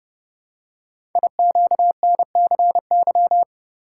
Morse code, heard as a keyed tone, about 28 wpm, 700 Hz. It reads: IQNCY
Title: Code Groups - 5 Letters - 30wpm